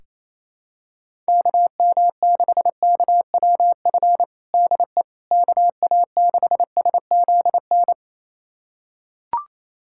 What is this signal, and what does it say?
Morse code, heard as a keyed tone, about 28 wpm, 700 Hz. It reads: KM6KWF DE KA6SZN
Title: USA Callsign DE Callsign - 6 Characters - 28wpm